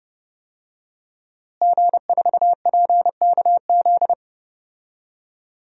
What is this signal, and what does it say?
Morse code, heard as a keyed tone, about 30 wpm, 700 Hz. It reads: G4PKZ
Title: DXCC-100 Call Signs - 5 Characters - Repeated 3x - 30wpm